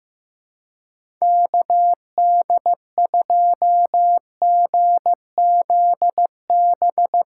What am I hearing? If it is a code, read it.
Morse code, heard as a keyed tone, about 15 wpm, 700 Hz. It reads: KD2GZB
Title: USA Call Signs - 6 Characters - 2x Character Spacing - 15wpm